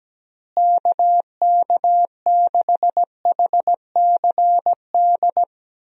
Morse code, heard as a keyed tone, about 17 wpm, 700 Hz. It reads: KK6HCD
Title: USA Call Signs - 6 Characters - Repeated 3x - 17wpm